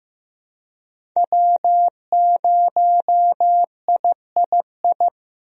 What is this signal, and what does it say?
Morse code, heard as a keyed tone, about 15 wpm, 700 Hz. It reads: W0III
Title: USA Call Signs - 5 Characters - Repeated 3x - 15wpm